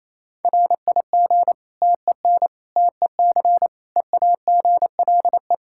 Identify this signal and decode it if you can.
Morse code, heard as a keyed tone, about 28 wpm, 700 Hz. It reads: RIG TEN TEC EAGLE
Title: Practice QSO - Exchange 2 - Info - 28wpm